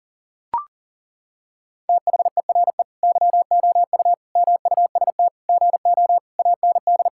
Morse code, heard as a keyed tone, about 37 wpm, 700 Hz. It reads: THERE YOU MUST GO AND
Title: Sentences from Top 1600 Words 40wpm